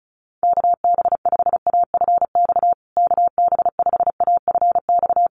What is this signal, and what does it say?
Morse code, heard as a keyed tone, about 34 wpm, 700 Hz. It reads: KB5AFX KB5AFX
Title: USA Call Signs Repeated 3x 35wpm